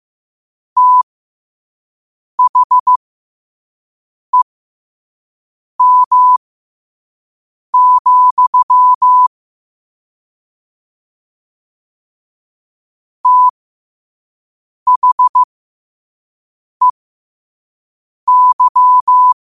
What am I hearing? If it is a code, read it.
Morse code, heard as a keyed tone, about 15 wpm, spaced as for 5 wpm, 1000 Hz. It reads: THEM, THEY
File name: qotdfiles / 5wpm